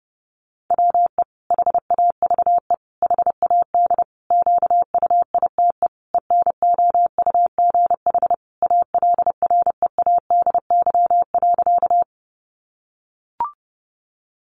Morse code, heard as a keyed tone, about 30 wpm, 700 Hz. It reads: WE HAVE HAD QUITE ENOUGH ALREADY.
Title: Sentences from Top 700 Words 30wpm